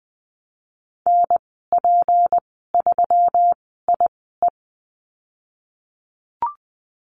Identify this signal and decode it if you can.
Morse code, heard as a keyed tone, about 19 wpm, spaced as for 14 wpm, 700 Hz. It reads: NP3IE
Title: 186 Lesson 30g - Letter b - Call Signs 20wpm